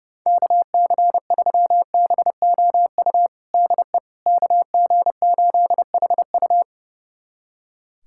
Morse code, heard as a keyed tone, about 30 wpm, 690 Hz. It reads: KC3BOU DE KG8HU
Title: USA Callsign DE Callsign 30wpm